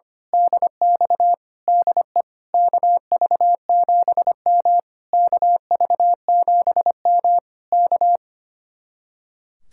Morse code, heard as a keyed tone, about 24 wpm, 700 Hz. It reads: DX DE KV7M KV7M K
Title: Speed-Racing Common QSO Phrases - 25wpm